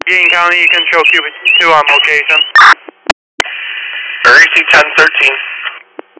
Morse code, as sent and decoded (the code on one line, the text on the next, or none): none